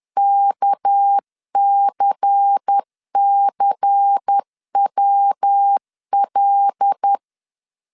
Morse code, heard as a keyed tone, about 10 wpm, 800 Hz. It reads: KCCWL